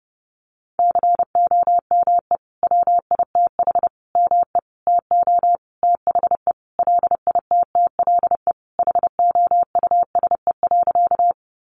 Morse code, heard as a keyed tone, about 30 wpm, 700 Hz. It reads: COME WITH ME TO THE LITTLE HOUSE.
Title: Sentences from Top 200 Words 30wpm